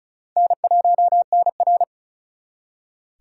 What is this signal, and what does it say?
Morse code, heard as a keyed tone, about 46 wpm, 700 Hz. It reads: N1NR